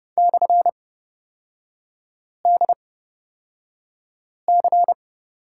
Morse code, heard as a keyed tone, about 30 wpm, spaced as for 5 wpm, 700 Hz. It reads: /DC
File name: Lesson 37e - Letter x - Three Characters 30wpm